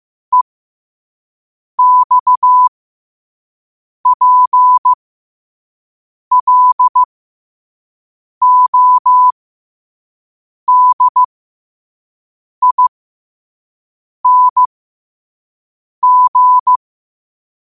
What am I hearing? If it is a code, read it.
Morse code, heard as a keyed tone, about 15 wpm, spaced as for 5 wpm, 1000 Hz. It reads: EXPLODING
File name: qotdfiles / 5wpm